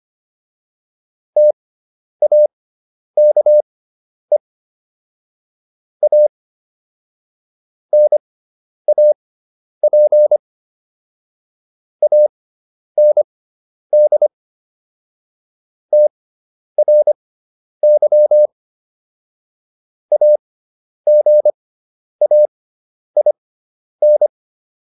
Morse code, heard as a keyed tone, about 25 wpm, spaced as for 10 wpm, 600 Hz. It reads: TAKE A NAP AND TRY AGAIN